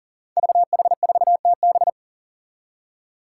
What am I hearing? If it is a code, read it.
Morse code, heard as a keyed tone, about 38 wpm, 700 Hz. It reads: VH4TB
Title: Five Challenging Letter-Number Combinations 40wpm